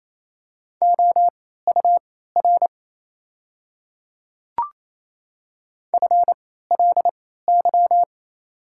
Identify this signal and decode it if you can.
Morse code, heard as a keyed tone, about 28 wpm, spaced as for 16 wpm, 700 Hz. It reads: OUR FLY
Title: 3x Character Spacing - Top 500 Words - 28wpm